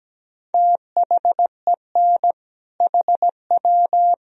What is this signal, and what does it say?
Morse code, heard as a keyed tone, about 17 wpm, 700 Hz. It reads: THEN HW
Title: Sentences from Top 100 Words and Abbreviations 17wpm